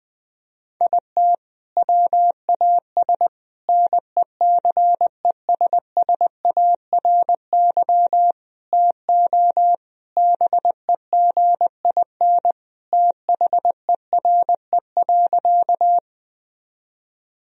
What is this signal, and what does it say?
Morse code, heard as a keyed tone, about 20 wpm, 700 Hz. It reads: IT WAS NECESSARY TO BEGIN THERE.
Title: Sentences from Top 1000 Words - Review Part 1 - 20wpm